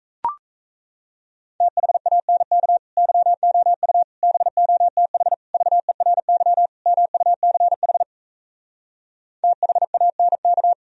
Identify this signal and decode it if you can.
Morse code, heard as a keyed tone, about 38 wpm, 700 Hz. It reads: THANK YOU BOTH VERY MUCH THANK